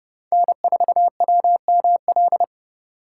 Morse code, heard as a keyed tone, about 29 wpm, 700 Hz. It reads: N4WML